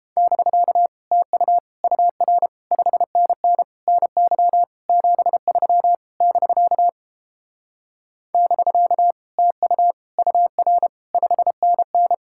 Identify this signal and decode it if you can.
Morse code, heard as a keyed tone, about 29 wpm, 700 Hz. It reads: <BK> TU UR 5NN NY 73 <BK> <BK> TU UR 5NN